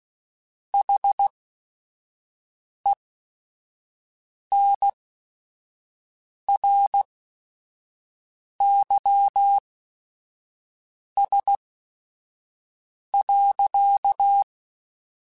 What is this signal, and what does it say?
Morse code, heard as a keyed tone, about 16 wpm, spaced as for 5 wpm, 800 Hz. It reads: HENRYS.
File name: Technical Subjects / Technical Subjects-mp3s / XL @5